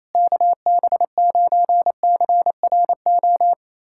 Morse code, heard as a keyed tone, about 28 wpm, 700 Hz. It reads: KB9CRO
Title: USA Callsign DE Callsign - 6 Characters - 28wpm